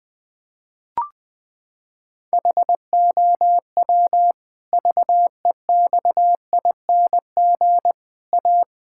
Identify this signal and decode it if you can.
Morse code, heard as a keyed tone, about 19 wpm, 700 Hz. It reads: HOW VEXING A
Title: Pangram Challenge v2 - 20wpm